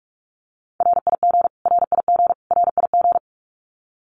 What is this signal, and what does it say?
Morse code, heard as a keyed tone, about 44 wpm, 700 Hz. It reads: RIG RIG RIG
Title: Abbreviations Repeated 3x - 45wpm